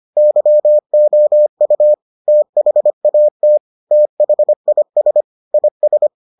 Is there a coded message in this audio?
Yes: Morse code from 0.2 to 6.1 s, heard as a keyed tone, about 25 wpm, 600 Hz. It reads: YOU THAT THIS IS